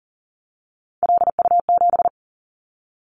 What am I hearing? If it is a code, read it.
Morse code, heard as a keyed tone, about 42 wpm, 700 Hz. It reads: LU7
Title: Callsign Prefixes for Top 50 DXCC - 40wpm